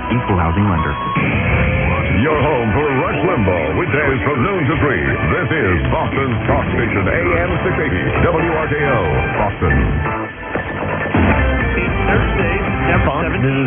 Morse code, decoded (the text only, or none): none